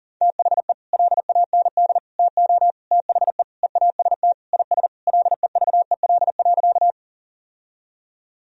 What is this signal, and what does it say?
Morse code, heard as a keyed tone, about 40 wpm, 700 Hz. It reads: THE LAND TO THE EAST IS LEVEL.
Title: Sentences from Top 1000 Words - Review Part 4 - 40wpm